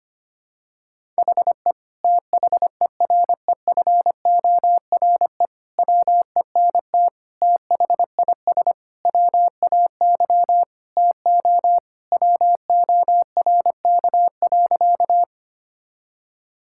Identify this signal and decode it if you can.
Morse code, heard as a keyed tone, about 25 wpm, 700 Hz. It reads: HE THEREFORE WENT THIS WAY TO WORK.
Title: Sentences from Top 1000 Words - Review Part 5 - 25wpm